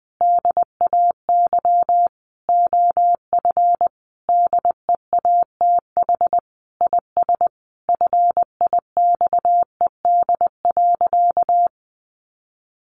Morse code, heard as a keyed tone, about 20 wpm, 700 Hz. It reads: DAY OF DEATH IS FIXED.
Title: Sentences from Top 1100 Words 20wpm